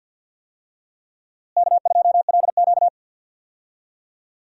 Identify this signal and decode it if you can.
Morse code, heard as a keyed tone, about 53 wpm, 700 Hz. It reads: K2LQ
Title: USA Call Signs - 4 Characters - Repeated 3x - 50wpm